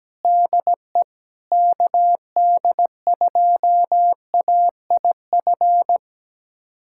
Morse code, heard as a keyed tone, about 17 wpm, 700 Hz. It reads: DE KD2AIF
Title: USA Callsign DE Callsign - 6 Characters - 17wpm